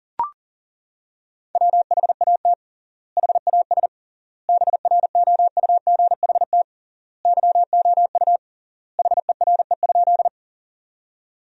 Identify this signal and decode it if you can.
Morse code, heard as a keyed tone, about 40 wpm, 700 Hz. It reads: WHAT HAS BROUGHT YOU HERE?
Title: 3x Word-Spacing - Sentences from Top 500 Words - 40wpm